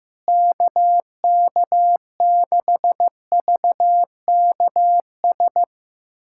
Morse code, heard as a keyed tone, about 15 wpm, 700 Hz. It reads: KK6VKS
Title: USA Callsign DE Callsign - 6 Characters - 15wpm